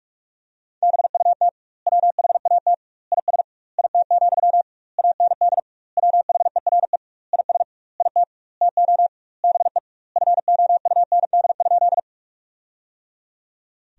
Morse code, heard as a keyed tone, about 46 wpm, 700 Hz. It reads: BUT WHAT IS IT, AND WHERE IS IT TO BE FOUND?
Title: 2x Word-Spacing - Sentences from Top 300 Words - 45wpm